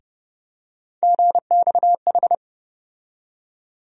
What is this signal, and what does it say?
Morse code, heard as a keyed tone, about 30 wpm, 700 Hz. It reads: GXH
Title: USA Call Sign Postfixes - 3 Letters 30wpm